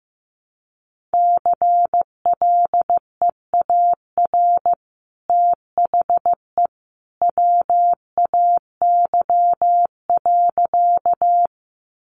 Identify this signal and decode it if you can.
Morse code, heard as a keyed tone, about 15 wpm, 700 Hz. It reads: CLEAR THE WAY.
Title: Sentences from Top 500 Words 15wpm